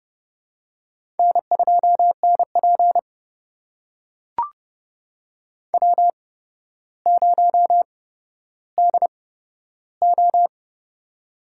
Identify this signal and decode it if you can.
Morse code, heard as a keyed tone, about 30 wpm, spaced as for 13 wpm, 700 Hz. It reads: N2NP W0DO
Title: USA Call Signs - 4 Characters - 8x Character Spacing - 30wpm